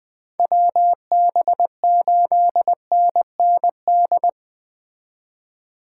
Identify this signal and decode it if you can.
Morse code, heard as a keyed tone, about 20 wpm, 700 Hz. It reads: WB8NND